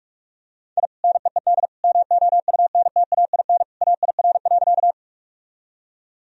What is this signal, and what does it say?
Morse code, heard as a keyed tone, about 46 wpm, 700 Hz. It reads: I NEED MOUNTAIN AIR.
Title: Sentences from Top 1000 Words - Review Part 1 - 45wpm